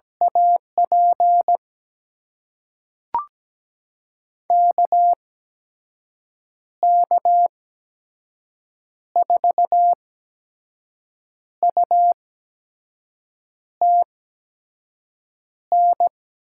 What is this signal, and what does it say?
Morse code, heard as a keyed tone, about 17 wpm, spaced as for 5 wpm, 700 Hz. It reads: AP KK4UTN
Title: USA Call Signs - 6 Characters - 8x Character Spacing - 17wpm